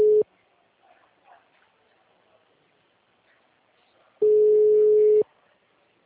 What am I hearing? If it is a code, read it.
Ringback tone (Europe, 425 Hz).
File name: out- -2040- - - .30975